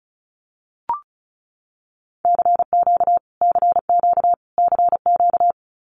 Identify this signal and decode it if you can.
Morse code, heard as a keyed tone, about 34 wpm, 700 Hz. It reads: CQ CQ CQ
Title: Abbreviations Repeated 3x - 35wpm